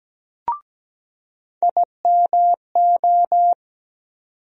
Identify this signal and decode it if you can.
Morse code, heard as a keyed tone, about 17 wpm, 700 Hz. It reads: IMO
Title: USA Call Sign Postfixes - 3 Letters 17wpm